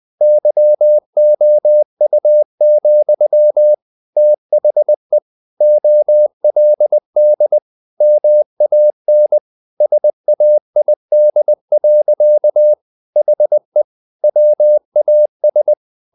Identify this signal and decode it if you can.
Morse code, heard as a keyed tone, about 20 wpm, 600 Hz. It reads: YOU, THE OLD MAN SAID. HE WAS
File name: the old man and the sea / the old man and the sea part 1 20wpm